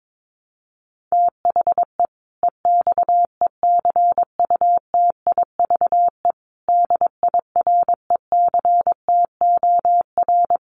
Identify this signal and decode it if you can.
Morse code, heard as a keyed tone, about 22 wpm, 700 Hz. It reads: THE EXECUTIVE DIRECTOR
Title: Sets of 5 Words Encore 22wpm